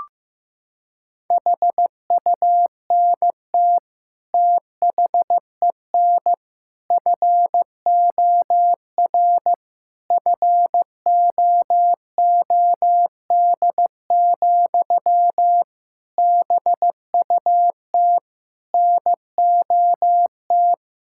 Morse code, heard as a keyed tone, about 15 wpm, 700 Hz. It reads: HUNT THEN FOR FOOD, BUT NOT
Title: Sentences from Top 1600 Words 15wpm